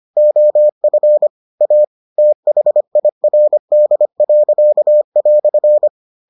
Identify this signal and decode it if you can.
Morse code, heard as a keyed tone, about 25 wpm, 600 Hz. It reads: OF A THIRD."